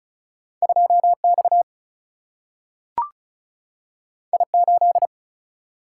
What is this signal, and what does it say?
Morse code, heard as a keyed tone, about 36 wpm, 700 Hz. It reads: 2X I8
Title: Single-Letter Single-Number 35wpm